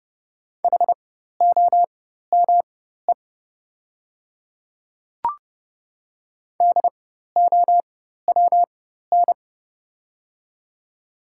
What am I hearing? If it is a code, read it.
Morse code, heard as a keyed tone, about 30 wpm, spaced as for 14 wpm, 700 Hz. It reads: HOME DOWN
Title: 4x Character Spacing - Top 200 Words - 30wpm